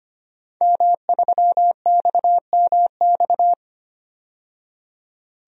Morse code, heard as a keyed tone, about 25 wpm, 700 Hz. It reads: M3XMX
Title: DXCC-100 Call Signs - 5 Characters - 25wpm